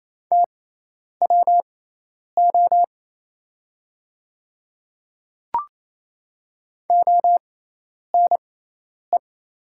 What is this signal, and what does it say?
Morse code, heard as a keyed tone, about 28 wpm, spaced as for 10 wpm, 700 Hz. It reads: TWO ONE
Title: 6x Character Spacing - Top 100 Words - 28wpm